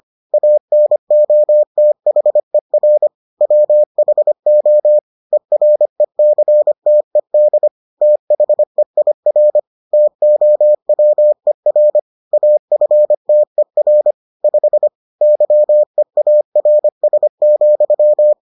Morse code, heard as a keyed tone, about 25 wpm, 600 Hz. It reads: ANOTHER WHO ERECTED THEIR TOWER AFTER 5 YEARS,